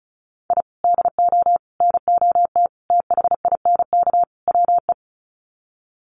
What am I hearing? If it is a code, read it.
Morse code, heard as a keyed tone, about 34 wpm, 700 Hz. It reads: I DO NOT THINK WE
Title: Sets of 5 Words 35wpm